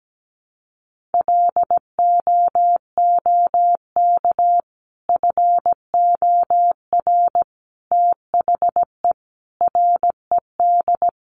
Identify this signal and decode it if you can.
Morse code, heard as a keyed tone, about 17 wpm, 700 Hz. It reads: LOOK FOR THE RED
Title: A Sound Of Thunder 17wpm